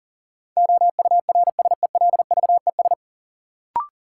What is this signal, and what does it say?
Morse code, heard as a keyed tone, about 40 wpm, 700 Hz. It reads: OURSELVES
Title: Sentences from Top 1600 Words - New Word Drill Alt - 40wpm